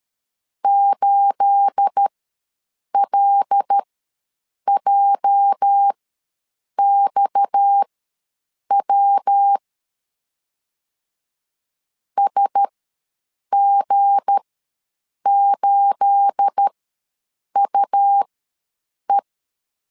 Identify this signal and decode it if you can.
Morse code, heard as a keyed tone, about 13 wpm, spaced as for 7 wpm, 800 Hz. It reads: 8LJXW SG8UE
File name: LEZIONI / 9 28 ws 12wpm 3354